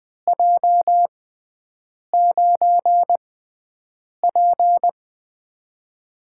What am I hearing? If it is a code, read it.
Morse code, heard as a keyed tone, about 20 wpm, spaced as for 7 wpm, 700 Hz. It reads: J9P
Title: USA Call Signs - 4 Characters - 6x Character Spacing - 20wpm